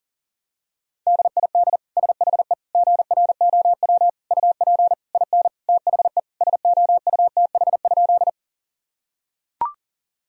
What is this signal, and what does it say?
Morse code, heard as a keyed tone, about 40 wpm, 700 Hz. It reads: DID SHE GROW UP IN THE SOUTH?
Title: Sentences from Top 1000 Words - Review Part 2 - 40wpm